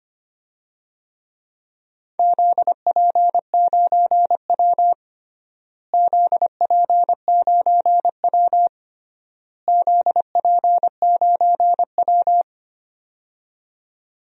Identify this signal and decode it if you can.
Morse code, heard as a keyed tone, about 25 wpm, 700 Hz. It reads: ZP9W ZP9W ZP9W
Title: DXCC-100 Call Signs - 4 Characters - Repeated 3x - 25wpm